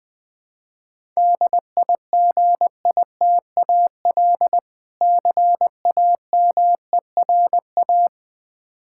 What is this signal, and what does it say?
Morse code, heard as a keyed tone, about 21 wpm, 700 Hz. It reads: DIGITAL CAMERA
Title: Difficult Words with Hints - 20wpm